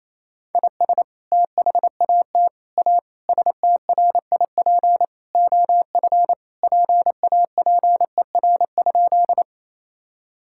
Morse code, heard as a keyed tone, about 28 wpm, 700 Hz. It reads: IS THAT A STRIP OF PAPER?
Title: Sentences from Top 1600 Words 28wpm